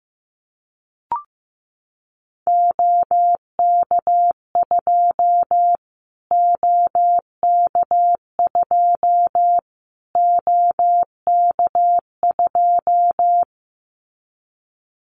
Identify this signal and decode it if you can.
Morse code, heard as a keyed tone, about 15 wpm, 700 Hz. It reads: OK2 OK2 OK2
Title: Callsign Prefixes for Top 25 DXCC - Repeated 3x - 15wpm